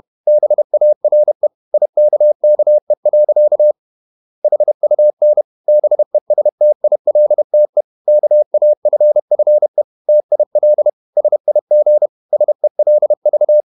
Morse code, heard as a keyed tone, about 31 wpm, 600 Hz. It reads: BARE IKKE. HUN BESTILTE KAFFE TIL SIG SELV
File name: Ferie QRQ 31wpm